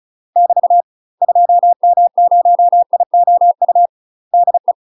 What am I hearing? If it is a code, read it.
Morse code, heard as a keyed tone, about 36 wpm, 700 Hz. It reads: <BT> 2M0IOU DE
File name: UK MASTER QSOs TRAINING / 35WPM / UK MASTER QSOs 0033